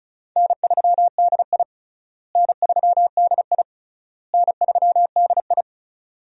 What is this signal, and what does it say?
Morse code, heard as a keyed tone, about 34 wpm, 700 Hz. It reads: N3DI N3DI N3DI
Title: USA Call Signs - 4 Characters - Repeated 3x - 35wpm